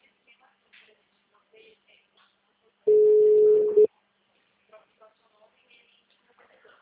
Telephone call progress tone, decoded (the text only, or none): ringback tone